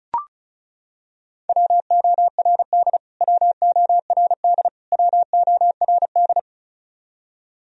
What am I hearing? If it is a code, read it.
Morse code, heard as a keyed tone, about 34 wpm, 700 Hz. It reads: WORD WORD WORD
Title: Top 100 Words Repeated 35wpm